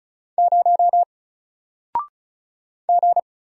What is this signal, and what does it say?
Morse code, heard as a keyed tone, about 34 wpm, 700 Hz. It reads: 0 G
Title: Single Letter-Number Alt 35wpm